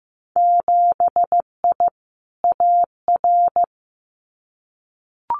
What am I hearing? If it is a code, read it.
Morse code, heard as a keyed tone, about 15 wpm, 700 Hz. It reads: 7I AR
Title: Winter Field Day Exchanges Encore 15wpm